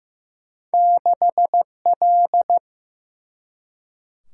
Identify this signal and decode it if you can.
Morse code, heard as a keyed tone, about 15 wpm, 700 Hz. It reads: 6L
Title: Single-Letter Single-Number 15wpm